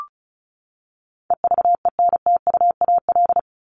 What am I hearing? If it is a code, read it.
Morse code, heard as a keyed tone, about 34 wpm, 700 Hz. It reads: EVENTUAL
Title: LY Suffix - Top 100 Words with Base Words 35wpm